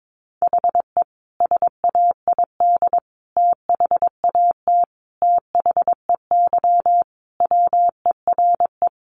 Morse code, heard as a keyed tone, about 22 wpm, 700 Hz. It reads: HE SAID THAT THEY WERE
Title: Sentences from Top 800 Words 22wpm